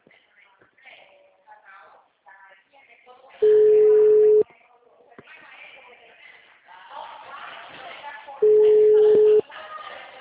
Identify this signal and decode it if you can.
Ringback tone (Europe, 425 Hz), with speech.
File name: out- -2041- - - .57191